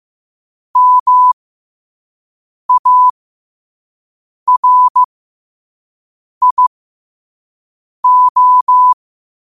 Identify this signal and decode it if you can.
Morse code, heard as a keyed tone, about 15 wpm, spaced as for 5 wpm, 1000 Hz. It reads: MARIO
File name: qotdfiles / 5wpm